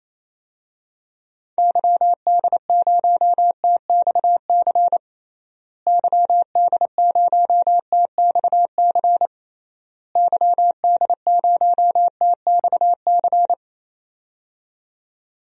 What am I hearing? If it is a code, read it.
Morse code, heard as a keyed tone, about 28 wpm, 700 Hz. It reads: YD0TXC YD0TXC YD0TXC
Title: DXCC-100 Call Signs - 6 Characters - Repeated 3x - 28wpm